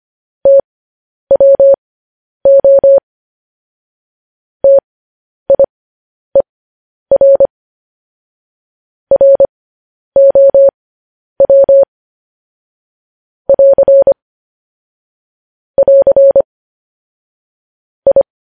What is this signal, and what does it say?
Morse code, heard as a keyed tone, about 25 wpm, spaced as for 10 wpm, 550 Hz. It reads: TWO TIER ROW <AR> <AR> I
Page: news-headlines-in-morse-code-at-10-wpm